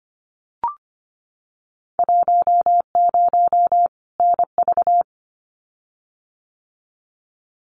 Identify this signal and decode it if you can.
Morse code, heard as a keyed tone, about 25 wpm, 690 Hz. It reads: 10 NV
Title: Winter Field Day Exchanges 25wpm